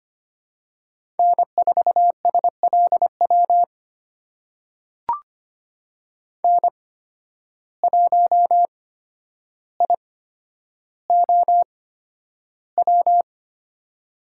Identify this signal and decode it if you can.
Morse code, heard as a keyed tone, about 25 wpm, spaced as for 11 wpm, 700 Hz. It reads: N4SLW N1IOW
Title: USA Call Signs - 5 Characters - 8x Character Spacing - 25wpm